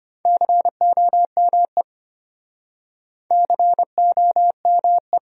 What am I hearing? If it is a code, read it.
Morse code, heard as a keyed tone, about 33 wpm, 700 Hz. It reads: COME COME